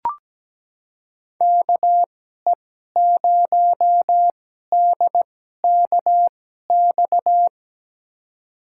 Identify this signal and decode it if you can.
Morse code, heard as a keyed tone, about 17 wpm, spaced as for 12 wpm, 700 Hz. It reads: KE0DKX
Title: USA Call Signs - 6 Characters - 2x Character Spacing - 17wpm